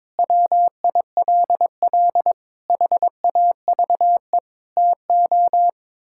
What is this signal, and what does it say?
Morse code, heard as a keyed tone, about 21 wpm, 700 Hz. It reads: WILL HAVE TO
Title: Sentences from Top 1000 Words - Review Part 6 - 22wpm